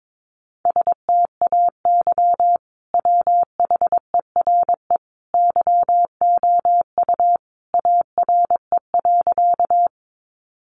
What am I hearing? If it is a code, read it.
Morse code, heard as a keyed tone, about 22 wpm, 700 Hz. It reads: STAY WHERE YOU ARE.
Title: Sentences from Top 500 Words 22wpm